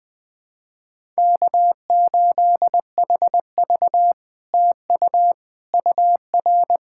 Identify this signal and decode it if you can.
Morse code, heard as a keyed tone, about 20 wpm, 700 Hz. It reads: K8HV TU UR
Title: SOTA QSO Practice 20wpm